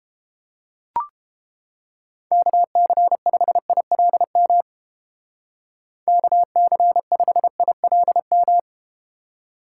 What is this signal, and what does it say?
Morse code, heard as a keyed tone, about 31 wpm, 700 Hz. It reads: KC5ILM KC5ILM